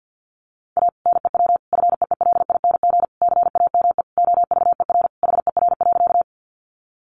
Morse code, heard as a keyed tone, about 50 wpm, 700 Hz. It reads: A NEW FEELING CAME OVER HER.
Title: Sentences from Top 1200 Words 50wpm